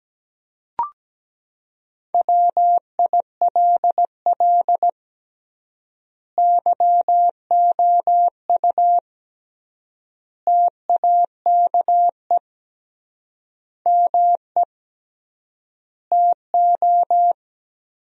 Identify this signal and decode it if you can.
Morse code, heard as a keyed tone, about 17 wpm, 700 Hz. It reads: WILL YOU TAKE ME TO
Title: 3x Word-Spacing - Sentences from Top 200 Words - 17wpm